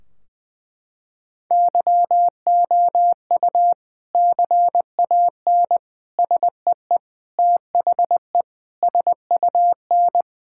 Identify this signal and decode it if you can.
Morse code, heard as a keyed tone, about 20 wpm, 700 Hz. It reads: YOU CAN SEE THE SUN
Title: Sentences from Top 1600 Words 20wpm